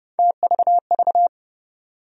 Morse code, heard as a keyed tone, about 30 wpm, 700 Hz. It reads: TVV
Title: USA Call Sign Postfixes - 3 Letters 30wpm